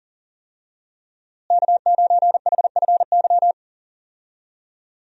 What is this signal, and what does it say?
Morse code, heard as a keyed tone, about 40 wpm, 700 Hz. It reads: K9HFY
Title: USA Call Signs - 5 Characters - Repeated 2x - 40wpm